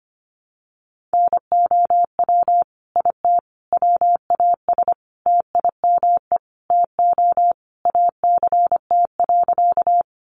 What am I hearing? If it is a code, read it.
Morse code, heard as a keyed tone, about 25 wpm, 700 Hz. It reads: NOW IT WAS TIME TO ACT.
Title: Sentences from Top 200 Words 25wpm